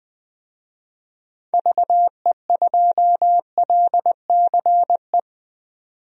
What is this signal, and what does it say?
Morse code, heard as a keyed tone, about 20 wpm, 700 Hz. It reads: VE2LCE
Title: DXCC-100 Call Signs - 6 Characters - 20wpm